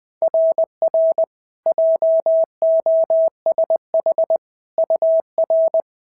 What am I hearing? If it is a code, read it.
Morse code, heard as a keyed tone, about 18 wpm, 650 Hz. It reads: RR JOSH UR
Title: SOTA QSO Practice 20wpm